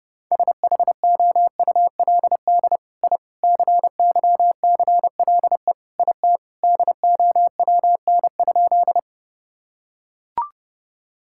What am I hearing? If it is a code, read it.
Morse code, heard as a keyed tone, about 28 wpm, 700 Hz. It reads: SHOULD I CYCLE IT DOWN?
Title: Sentences from Top 1000 Words - Review Part 6 - 30wpm